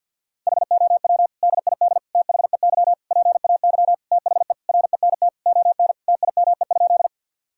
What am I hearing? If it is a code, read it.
Morse code, heard as a keyed tone, about 50 wpm, 700 Hz. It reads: HOW DID THEY PAY THE RENT ON TIME?
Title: Sentences from Top 1000 Words - Review Part 5 - 50wpm